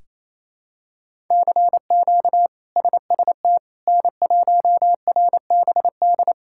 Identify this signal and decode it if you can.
Morse code, heard as a keyed tone, about 28 wpm, 700 Hz. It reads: CQ SST N1RBD
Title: K1USN SST QSO Practice - 28wpm